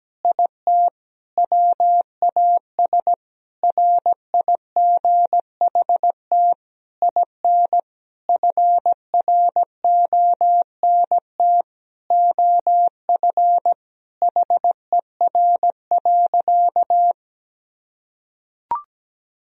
Morse code, heard as a keyed tone, about 17 wpm, 700 Hz. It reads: IT WAS RIGHT IN FRONT OF HER.
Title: Sentences from Top 1000 Words - Review Part 2 - 17wpm